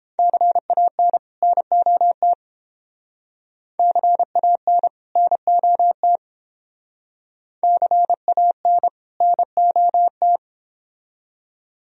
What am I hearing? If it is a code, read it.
Morse code, heard as a keyed tone, about 28 wpm, 700 Hz. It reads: CAN NOT CAN NOT CAN NOT